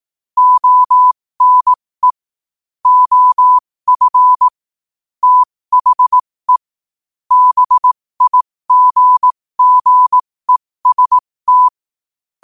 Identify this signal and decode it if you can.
Morse code, heard as a keyed tone, about 18 wpm, spaced as for 15 wpm, 1000 Hz. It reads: ONE OF THE BIGGEST
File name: qotdfiles / 15wpm